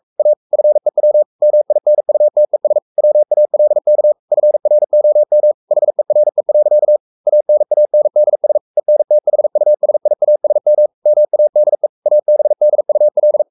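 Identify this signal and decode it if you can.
Morse code, heard as a keyed tone, about 44 wpm, 600 Hz. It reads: A FEW MINUTES WALK FROM HERE. ANANDS ENTHUSIASM MADE ABDUL
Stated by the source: Morse code at 43 wpm sending Ken Follett's novel Never in English, from CW med Gnister.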